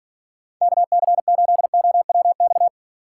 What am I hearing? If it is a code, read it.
Morse code, heard as a keyed tone, about 47 wpm, 700 Hz. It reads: KC8OWX